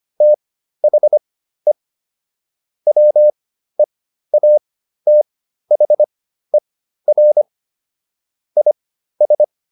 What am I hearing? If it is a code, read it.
Morse code, heard as a keyed tone, about 25 wpm, spaced as for 13 wpm, 600 Hz. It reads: THE WEATHER IS